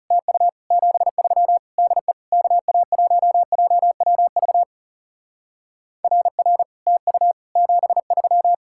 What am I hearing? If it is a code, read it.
Morse code, heard as a keyed tone, about 37 wpm, 700 Hz. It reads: TU 73 DE KA1JWV RR TU 73
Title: Speed-Racing Common QSO Phrases Encore - 30wpm